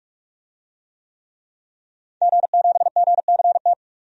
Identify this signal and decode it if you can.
Morse code, heard as a keyed tone, about 42 wpm, 700 Hz. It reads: G7GCT
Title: DXCC-100 Call Signs - 5 Characters - 45wpm